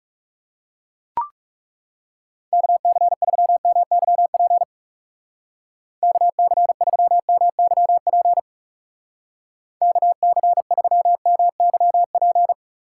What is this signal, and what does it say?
Morse code, heard as a keyed tone, about 36 wpm, 700 Hz. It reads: KC3MYP KC3MYP KC3MYP
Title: Speed Racing - USA 2x3 Call Signs 30wpm